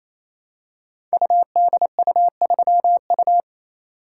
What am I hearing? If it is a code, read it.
Morse code, heard as a keyed tone, about 27 wpm, 700 Hz. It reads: UDU3U
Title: Code Groups - 5 Letter:Numbers - 28wpm